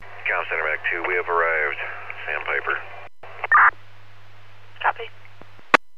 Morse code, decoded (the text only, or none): none